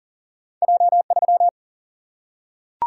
Morse code, heard as a keyed tone, about 42 wpm, 700 Hz. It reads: J3
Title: Single-Letter Single-Number 40wpm